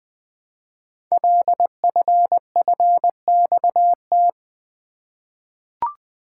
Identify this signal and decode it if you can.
Morse code, heard as a keyed tone, about 20 wpm, 700 Hz. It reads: LFFXT
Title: Code Groups - 5 Letters - 20wpm